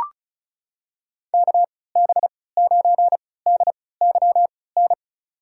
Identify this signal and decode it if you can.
Morse code, heard as a keyed tone, about 36 wpm, spaced as for 20 wpm, 700 Hz. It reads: KB9DYN
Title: USA Call Signs - 3x Character Spacing - 35wpm